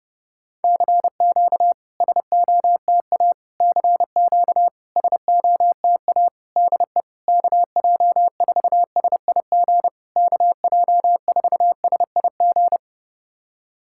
Morse code, heard as a keyed tone, about 29 wpm, 690 Hz. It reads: CQ SOTA CQ SOTA DE KJ4SIG KJ4SIG
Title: SOTA QSO Practice 30wpm